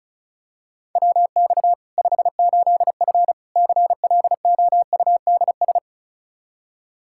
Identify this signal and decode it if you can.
Morse code, heard as a keyed tone, about 36 wpm, 700 Hz. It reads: WX 58F CLOUDS
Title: Practice QSO - Exchange 2 - Info - 35wpm